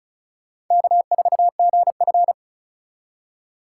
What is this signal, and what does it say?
Morse code, heard as a keyed tone, about 42 wpm, 700 Hz. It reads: K4GF
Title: Speed Racing - USA 1x2 Call Signs 35wpm